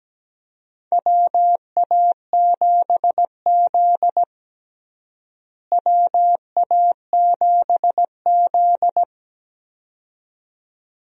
Morse code, heard as a keyed tone, about 17 wpm, 700 Hz. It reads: WA7Z WA7Z
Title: USA Call Signs - 4 Characters - Repeated 2x - 17wpm